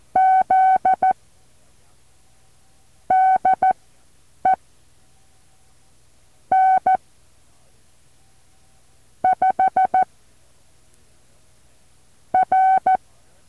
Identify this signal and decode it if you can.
Morse code, heard as a keyed tone, about 14 wpm, spaced as for 4 wpm, 760 Hz. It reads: Z DE N5R
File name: intl morse code2